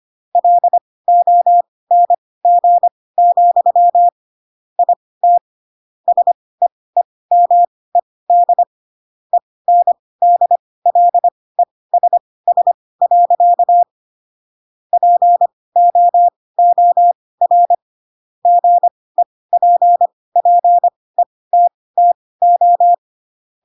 Morse code, heard as a keyed tone, about 25 wpm, spaced as for 18 wpm, 700 Hz. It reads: LONG, IT SEEMED ENDLESS. POOR GEPPETTO
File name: Pinocchio ch3 12 25 ik1hge